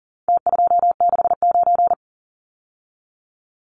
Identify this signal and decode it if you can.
Morse code, heard as a keyed tone, about 40 wpm, 700 Hz. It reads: T269
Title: Numbers - Contest Style - 4-Digit 40wpm